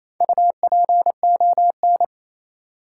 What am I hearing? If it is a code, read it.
Morse code, heard as a keyed tone, about 28 wpm, 700 Hz. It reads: UPON
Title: Sentences from Top 1100 Words - New Word Drill Alt - 28wpm